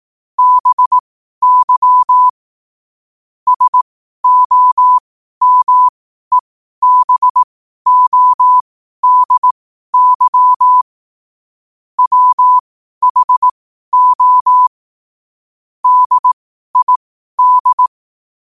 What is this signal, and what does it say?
Morse code, heard as a keyed tone, about 18 wpm, spaced as for 12 wpm, 1000 Hz. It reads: BY SOMEBODY WHO DID
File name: qotdfiles / 12wpm